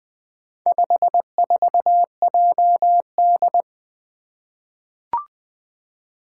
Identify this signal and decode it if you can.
Morse code, heard as a keyed tone, about 20 wpm, 700 Hz. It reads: 54JD
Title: Four Challenging Letter-Number Combinations 20wpm